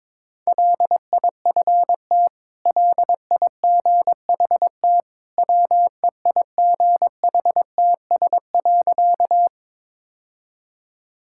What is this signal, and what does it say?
Morse code, heard as a keyed tone, about 22 wpm, 700 Hz. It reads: LIFT LIGHT WEIGHTS.
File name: practice / 22wpm / Easy Sentences with Hints - 22wpm